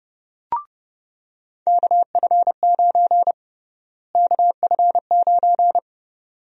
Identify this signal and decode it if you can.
Morse code, heard as a keyed tone, about 30 wpm, 700 Hz. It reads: KF9 KF9
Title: USA Call Sign Prefixes - Repeated 2x - 30wpm